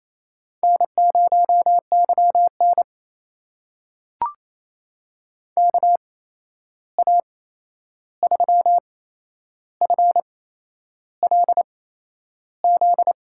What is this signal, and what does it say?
Morse code, heard as a keyed tone, about 28 wpm, spaced as for 11 wpm, 700 Hz. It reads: N0YN KA3FLZ
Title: USA Call Signs - 8x Character Spacing - 28wpm